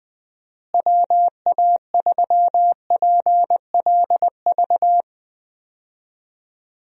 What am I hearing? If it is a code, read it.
Morse code, heard as a keyed tone, about 20 wpm, 700 Hz. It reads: WA3PLV
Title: USA Call Signs - 6 Characters - Repeated 2x - 20wpm